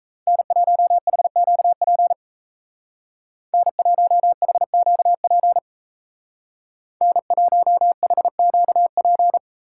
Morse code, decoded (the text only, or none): N1HQP N1HQP N1HQP